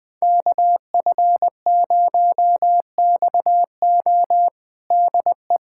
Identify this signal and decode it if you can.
Morse code, heard as a keyed tone, about 20 wpm, 700 Hz. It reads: KF0XO DE
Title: USA Callsign DE Callsign - 5 Characters - 20wpm